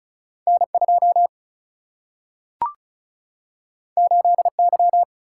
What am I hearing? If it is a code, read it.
Morse code, heard as a keyed tone, about 36 wpm, 700 Hz. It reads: N 2 8 Y
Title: Single-Letter Single-Number 35wpm